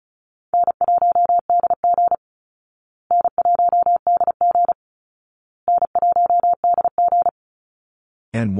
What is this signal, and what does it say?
Morse code, heard as a keyed tone, about 34 wpm, 700 Hz. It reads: N1DG N1DG N1DG
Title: Call Signs Repeated 2x - 35wpm